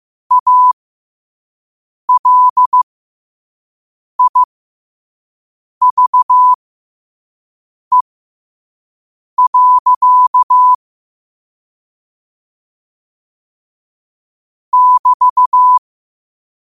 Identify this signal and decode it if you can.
Morse code, heard as a keyed tone, about 15 wpm, spaced as for 5 wpm, 1000 Hz. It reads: ALIVE. <BT>
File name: qotdfiles / 5wpm